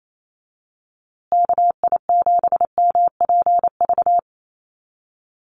Morse code, heard as a keyed tone, about 28 wpm, 700 Hz. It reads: KI7MPV
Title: USA Call Signs Repeated 3x 28wpm